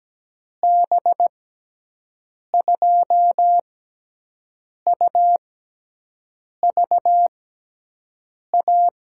Morse code, heard as a keyed tone, about 17 wpm, spaced as for 6 wpm, 700 Hz. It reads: B2UVA
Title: USA Call Signs - 6 Characters - 6x Character Spacing - 17wpm